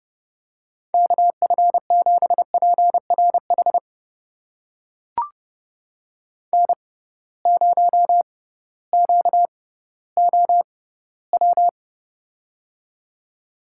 Morse code, heard as a keyed tone, about 30 wpm, spaced as for 16 wpm, 700 Hz. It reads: KF7PRH N0QOW
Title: USA Call Signs - 6x Character Spacing - 30wpm